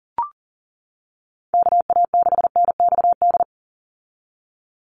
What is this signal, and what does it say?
Morse code, heard as a keyed tone, about 40 wpm, 700 Hz. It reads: KA6NXD
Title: USA Call Signs 40wpm